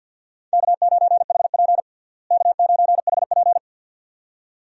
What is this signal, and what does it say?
Morse code, heard as a keyed tone, about 47 wpm, 700 Hz. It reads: K9HP K9HP
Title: USA Call Signs - 4 Characters - Repeated 2x - 50wpm